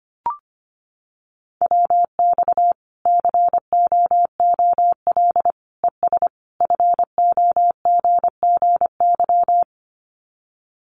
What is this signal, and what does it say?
Morse code, heard as a keyed tone, about 25 wpm, 700 Hz. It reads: WX COOL ES FOGGY
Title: Common QSO Phrases 25wpm